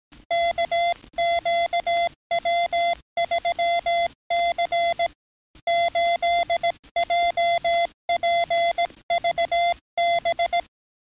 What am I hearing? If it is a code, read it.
Morse code, heard as a keyed tone, about 17 wpm, 690 Hz. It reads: KQW3C 8JPVB
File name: training / NAVPERS 11049to11060 / NAVPERS 11057 RA RB